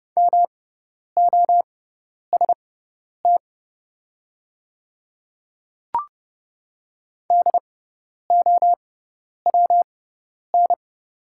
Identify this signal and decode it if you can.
Morse code, heard as a keyed tone, about 30 wpm, spaced as for 10 wpm, 700 Hz. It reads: MOST DOWN
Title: 6x Character Spacing - Top 300 Words - 30wpm